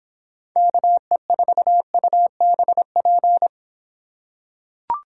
Morse code, heard as a keyed tone, about 26 wpm, 700 Hz. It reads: KE4UBP